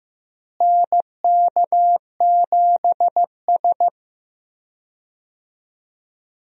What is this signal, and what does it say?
Morse code, heard as a keyed tone, about 15 wpm, 700 Hz. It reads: NK7S
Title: USA Call Signs - 4 Characters - Repeated 3x - 15wpm